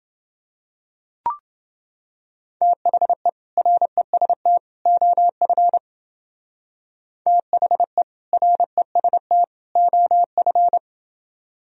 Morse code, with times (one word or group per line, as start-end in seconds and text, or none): 2.6-3.3 THE
3.6-4.6 REST
4.9-5.8 OF
7.3-8.0 THE
8.3-9.4 REST
9.8-10.8 OF